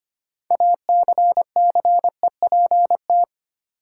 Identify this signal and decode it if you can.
Morse code, heard as a keyed tone, about 25 wpm, 700 Hz. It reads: ACCEPT
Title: Sentences from Top 1300 Words - New Word Drill - 25wpm